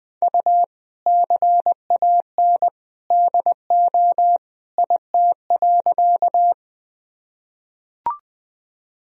Morse code, heard as a keyed tone, about 20 wpm, 700 Hz. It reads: U CAN DO IT.
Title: Sentences from Top 100 Words and Abbreviations 20wpm